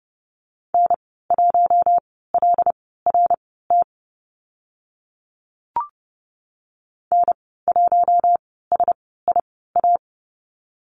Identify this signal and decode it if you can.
Morse code, heard as a keyed tone, about 30 wpm, spaced as for 17 wpm, 700 Hz. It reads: N1LRT N1SIA
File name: Lesson 14g - Number 2 - Call Signs 30wpm